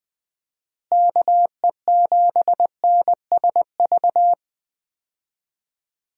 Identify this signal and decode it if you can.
Morse code, heard as a keyed tone, about 20 wpm, 700 Hz. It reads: KE7NSV
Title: USA Call Signs - 6 Characters - Repeated 2x - 20wpm